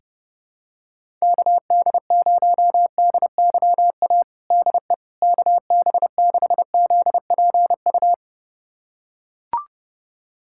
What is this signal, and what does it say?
Morse code, heard as a keyed tone, about 32 wpm, 700 Hz. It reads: KD0DYA DE KB6ZPU
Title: USA Callsign DE Callsign 30wpm